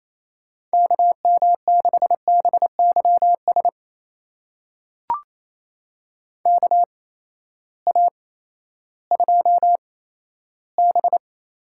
Morse code, heard as a keyed tone, about 28 wpm, spaced as for 14 wpm, 700 Hz. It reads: KM6BYS KA2B
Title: USA Call Signs - 6 Characters - 8x Character Spacing - 28wpm